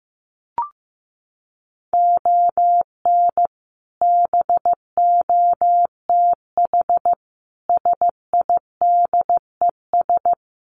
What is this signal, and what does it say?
Morse code, heard as a keyed tone, about 15 wpm, 700 Hz. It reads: ON BOTH SIDES
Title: Sets of 5 Words 15wpm